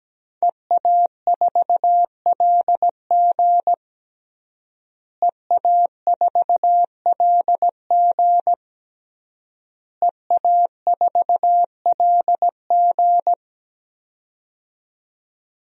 Morse code, heard as a keyed tone, about 17 wpm, 700 Hz. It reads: EA4LG EA4LG EA4LG
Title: DXCC-100 Call Signs - 5 Characters - Repeated 3x - 17wpm